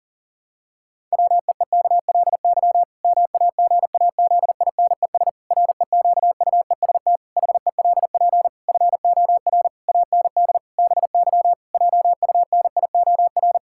Morse code, heard as a keyed tone, about 40 wpm, 700 Hz. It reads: WEEKLY MAGAZINES REQUEST HELP FOR AND BY JUNIOR
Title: Pangram Challenge v2 - 40wpm